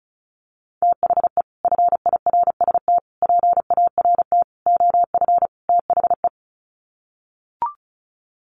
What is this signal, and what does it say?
Morse code, heard as a keyed tone, about 34 wpm, 700 Hz. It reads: THE FIRST PART OF THE
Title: Sets of 5 Words Encore 35wpm